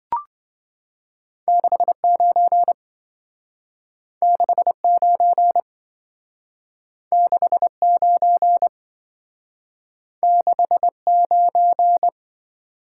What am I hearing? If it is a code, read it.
Morse code, heard as a keyed tone, about 26 wpm, 700 Hz. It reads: 69 69 69 69